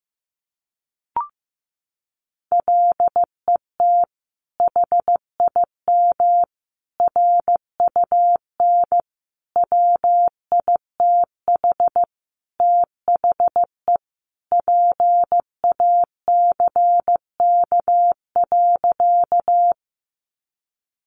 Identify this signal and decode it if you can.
Morse code, heard as a keyed tone, about 15 wpm, 700 Hz. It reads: LET HIM RUN WITH THE PACK.
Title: Sentences from Top 1300 Words 15wpm